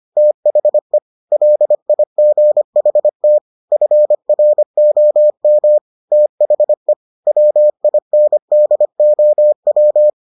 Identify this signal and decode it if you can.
Morse code, heard as a keyed tone, about 25 wpm, 600 Hz. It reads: THE LIGHT FROM THE WINDOW